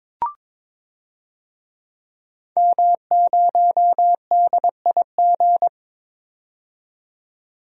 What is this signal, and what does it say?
Morse code, heard as a keyed tone, about 21 wpm, 700 Hz. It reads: M0DIG
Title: DXCC-100 Call Signs - 5 Characters - 22wpm